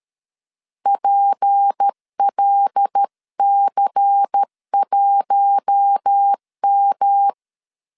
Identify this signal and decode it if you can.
Morse code, heard as a keyed tone, about 13 wpm, 800 Hz. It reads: PLC1M
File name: LEZIONI / 6 PKX0 / 6 P 12wpm 2033